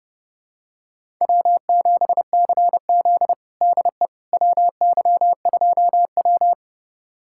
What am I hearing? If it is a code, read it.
Morse code, heard as a keyed tone, about 30 wpm, 700 Hz. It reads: W7CZ DE WY2W
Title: USA Callsign DE Callsign - 4 Characters - 30wpm